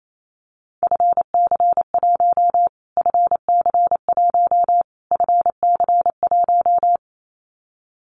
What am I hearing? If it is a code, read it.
Morse code, heard as a keyed tone, about 28 wpm, 700 Hz. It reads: FC1 FC1 FC1
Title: Callsign Prefixes for Top 25 DXCC - Repeated 3x - 28wpm